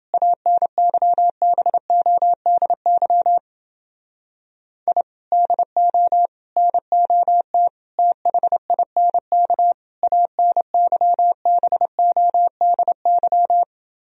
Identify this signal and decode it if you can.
Morse code, heard as a keyed tone, about 29 wpm, 700 Hz. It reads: ANYBODY I DO NOT THINK ANYBODY